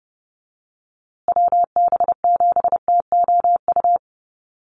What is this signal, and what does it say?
Morse code, heard as a keyed tone, about 28 wpm, 700 Hz. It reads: WB7TOU
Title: USA Call Signs Repeated 2x 30wpm